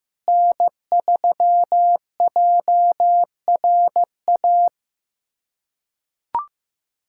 Morse code, heard as a keyed tone, about 15 wpm, 700 Hz. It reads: N3JRA
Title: USA Callsign DE Callsign - 5 Characters - 15wpm